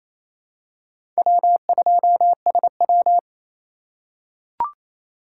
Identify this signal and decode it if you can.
Morse code, heard as a keyed tone, about 27 wpm, 700 Hz. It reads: W2SW
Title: USA Call Signs - 4 Characters - 3x Character Spacing - 28wpm